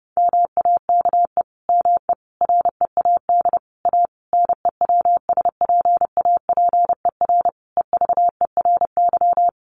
Morse code, heard as a keyed tone, about 30 wpm, 700 Hz. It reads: MAKE ME READ A NEWSPAPER EVERY
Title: Sentences from Top 1100 Words 30wpm